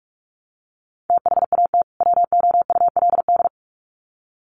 Morse code, heard as a keyed tone, about 44 wpm, 700 Hz. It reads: THAT WOULD
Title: Sets of 2 Words 45wpm